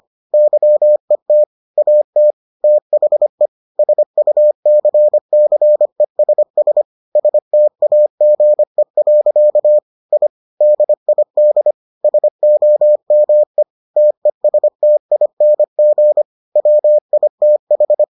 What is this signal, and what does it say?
Morse code, heard as a keyed tone, about 25 wpm, 600 Hz. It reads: YET AT THE SUCCESS STAGE. I DID SOME TESTING WITH